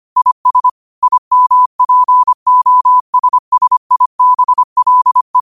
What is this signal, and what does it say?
Morse code, heard as a keyed tone, about 25 wpm, 1000 Hz. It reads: IS IMPOSSIBLE
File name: qotdfiles / 25wpm